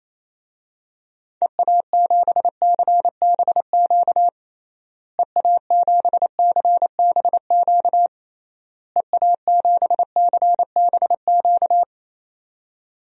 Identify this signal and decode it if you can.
Morse code, heard as a keyed tone, about 27 wpm, 700 Hz. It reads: EA7CBQ EA7CBQ EA7CBQ
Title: DXCC-100 Call Signs - 6 Characters - Repeated 3x - 28wpm